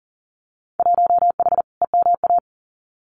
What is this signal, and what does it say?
Morse code, heard as a keyed tone, about 42 wpm, 700 Hz. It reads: 1H EMA
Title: Winter Field Day Exchanges 40wpm